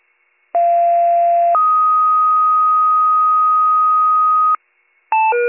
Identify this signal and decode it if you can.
Two-tone page, heard as a one-second tone, then a three-second tone, about 690 then 1250 Hz. It is heard from 0.5 to 4.6 s.